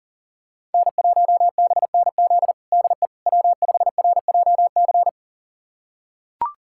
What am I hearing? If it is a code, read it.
Morse code, heard as a keyed tone, about 37 wpm, 700 Hz. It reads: N1BNZ DE W5RJC
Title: USA Callsign DE Callsign - 5 Characters - 40wpm